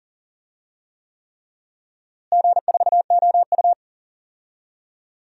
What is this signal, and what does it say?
Morse code, heard as a keyed tone, about 40 wpm, 700 Hz. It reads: G4OU
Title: DXCC-100 Call Signs - 4 Characters - Repeated 3x - 40wpm